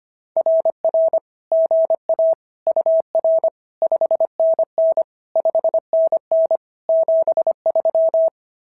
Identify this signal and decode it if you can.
Morse code, heard as a keyed tone, about 22 wpm, 650 Hz. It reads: RR GA UR 5NN 5NN 73
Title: SOTA QSO Practice 25wpm